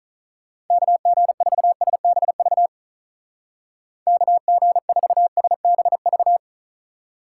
Morse code, heard as a keyed tone, about 46 wpm, 700 Hz. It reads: KG4SBV KG4SBV